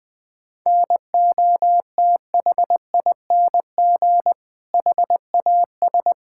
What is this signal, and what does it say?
Morse code, heard as a keyed tone, about 20 wpm, 700 Hz. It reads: NOTHING HAS
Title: Sentences from Top 1000 Words - Review Part 5 - 20wpm